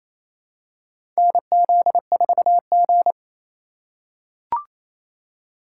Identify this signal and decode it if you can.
Morse code, heard as a keyed tone, about 28 wpm, 700 Hz. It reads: NZ4G
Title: USA Call Signs - 4 Characters - 2x Character Spacing - 28wpm